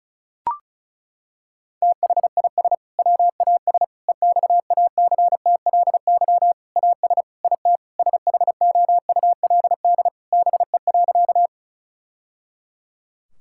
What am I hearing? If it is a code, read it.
Morse code, heard as a keyed tone, about 34 wpm, 700 Hz. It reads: THIS WAS EXACTLY AS IT SHOULD BE.
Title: Sentences from Top 1000 Words - Review Part 6 - 35wpm